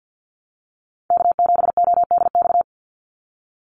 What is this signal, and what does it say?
Morse code, heard as a keyed tone, about 49 wpm, 700 Hz. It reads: K7ODX
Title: NA SOTA Chaser Callsigns 50wpm